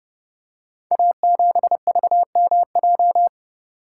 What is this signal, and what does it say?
Morse code, heard as a keyed tone, about 30 wpm, 700 Hz. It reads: A7VMJ
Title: Code Groups - 5 Letter:Numbers - 30wpm